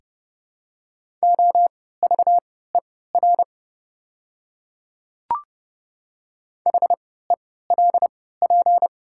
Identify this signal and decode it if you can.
Morse code, heard as a keyed tone, about 30 wpm, spaced as for 17 wpm, 700 Hz. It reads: OVER HELP
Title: 3x Character Spacing - Top 300 Words - 30wpm